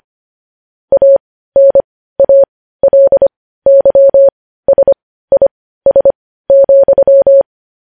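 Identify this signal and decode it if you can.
Morse code, heard as a keyed tone, about 25 wpm, spaced as for 15 wpm, 550 Hz. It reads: ANALYSIS,
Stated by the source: Morse code news headlines sent at 15, opening with vvv.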